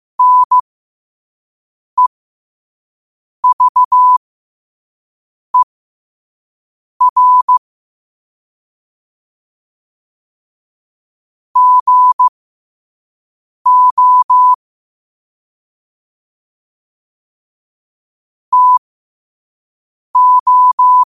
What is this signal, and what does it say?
Morse code, heard as a keyed tone, about 15 wpm, spaced as for 5 wpm, 1000 Hz. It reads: NEVER GO TO
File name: qotdfiles / 5wpm